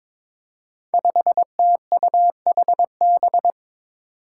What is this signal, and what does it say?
Morse code, heard as a keyed tone, about 22 wpm, 700 Hz. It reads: 5TUHB
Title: Five Challenging Letter-Number Combinations 22wpm